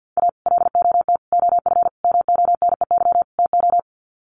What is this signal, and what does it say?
Morse code, heard as a keyed tone, about 50 wpm, 700 Hz. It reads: A LOT OF MONEY TO
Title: Sets of 5 Words Encore2 50wpm